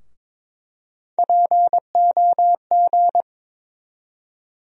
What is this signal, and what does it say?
Morse code, heard as a keyed tone, about 22 wpm, 700 Hz. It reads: POG
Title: USA Call Sign Postfixes - 3 Letters 22wpm